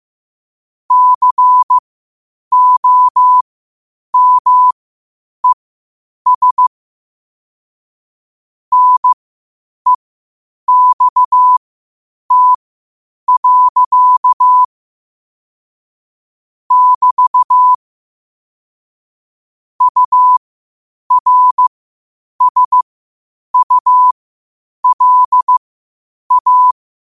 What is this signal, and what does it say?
Morse code, heard as a keyed tone, about 15 wpm, spaced as for 8 wpm, 1000 Hz. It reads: COMES NEXT. <BT> URSULA
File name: qotdfiles / 8wpm